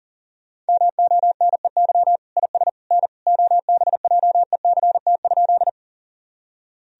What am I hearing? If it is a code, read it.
Morse code, heard as a keyed tone, about 40 wpm, 700 Hz. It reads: MONEY IS N OBJECT?
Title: Sentences from Top 500 Words and Abbreviations 40wpm